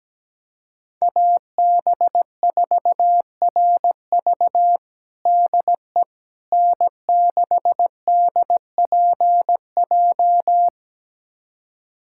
Morse code, heard as a keyed tone, about 17 wpm, 700 Hz. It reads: AB4RV DE N6DPJ
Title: USA Callsign DE Callsign - 5 Characters - 17wpm